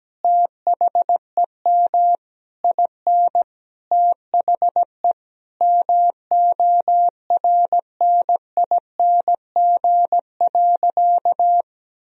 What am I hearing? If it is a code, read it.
Morse code, heard as a keyed tone, about 17 wpm, 700 Hz. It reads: THEM IN THE MORNING.
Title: Sentences from Top 1000 Words - Review Part 2 - 17wpm